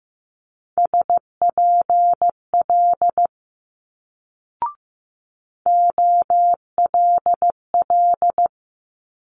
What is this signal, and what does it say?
Morse code, heard as a keyed tone, about 16 wpm, 700 Hz. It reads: SPL OLL
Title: Multi-letter Phonemes Quick Fire 15wpm